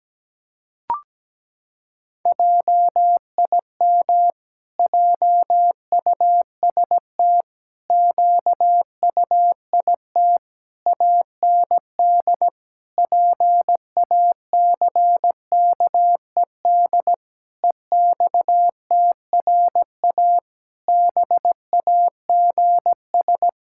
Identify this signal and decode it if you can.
Morse code, heard as a keyed tone, about 17 wpm, 700 Hz. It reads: JIM JUST QUIT AND PACKED EXTRA BAGS
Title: Pangram Challenge v2 - 17wpm